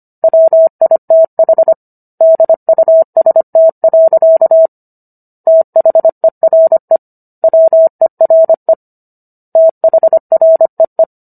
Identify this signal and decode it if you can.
Morse code, heard as a keyed tone, about 25 wpm, 650 Hz. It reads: WITH DUST. THERE WERE THREE